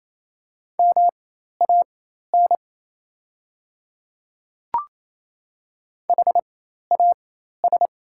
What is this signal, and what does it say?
Morse code, heard as a keyed tone, about 28 wpm, spaced as for 13 wpm, 700 Hz. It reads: MAN HAS
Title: 4x Character Spacing - Top 500 Words - 28wpm